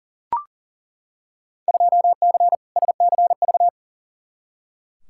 Morse code, heard as a keyed tone, about 40 wpm, 700 Hz. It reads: 2C SCV
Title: Field Day Exchanges Encore - 40wpm